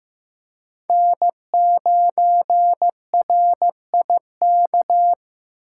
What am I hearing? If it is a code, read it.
Morse code, heard as a keyed tone, about 15 wpm, 700 Hz. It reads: N9RIK
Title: USA Call Signs - 5 Characters - 8x Character Spacing - 15wpm